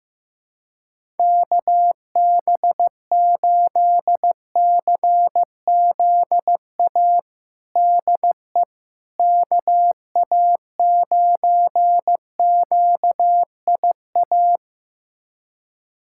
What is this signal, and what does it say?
Morse code, heard as a keyed tone, about 16 wpm, 700 Hz. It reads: KB8CZA DE KA9QIA
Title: USA Callsign DE Callsign 15wpm